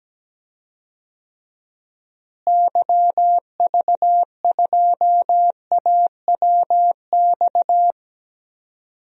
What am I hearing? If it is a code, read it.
Morse code, heard as a keyed tone, about 17 wpm, 700 Hz. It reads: YV2AWX
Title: DXCC-100 Call Signs - 6 Characters - Repeated 2x - 17wpm